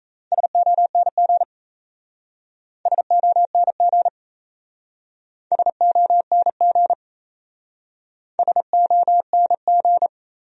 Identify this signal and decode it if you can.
Morse code, heard as a keyed tone, about 36 wpm, 700 Hz. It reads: SONG SONG SONG SONG